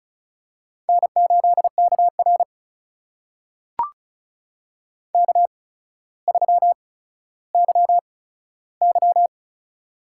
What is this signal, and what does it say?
Morse code, heard as a keyed tone, about 36 wpm, spaced as for 15 wpm, 700 Hz. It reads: N8KR K3YY
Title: USA Call Signs - 8x Character Spacing - 35wpm